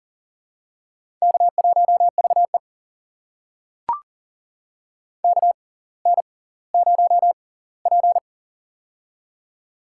Morse code, heard as a keyed tone, about 40 wpm, spaced as for 20 wpm, 700 Hz. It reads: K1VE KN0P
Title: USA Call Signs - 6x Character Spacing - 40wpm